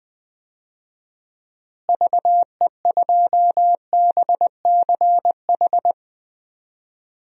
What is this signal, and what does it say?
Morse code, heard as a keyed tone, about 21 wpm, 700 Hz. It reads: VE2BCH
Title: DXCC-100 Call Signs - 6 Characters - Repeated 2x - 20wpm